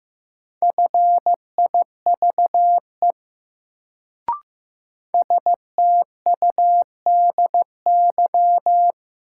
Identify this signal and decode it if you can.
Morse code, heard as a keyed tone, about 15 wpm, 700 Hz. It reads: FIVE STUDY
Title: Sets of 5 Words - New Word Drill - 15wpm